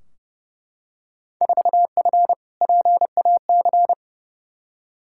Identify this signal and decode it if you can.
Morse code, heard as a keyed tone, about 31 wpm, 700 Hz. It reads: 4F PAC
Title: Field Day Exchanges Encore - 30wpm